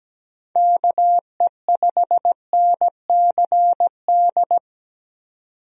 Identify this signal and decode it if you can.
Morse code, heard as a keyed tone, about 17 wpm, 700 Hz. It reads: KE5NCD
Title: USA Call Signs - 6 Characters - 17wpm